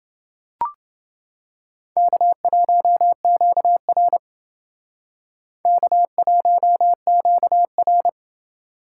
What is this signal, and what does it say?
Morse code, heard as a keyed tone, about 28 wpm, 700 Hz. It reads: K1QR K1QR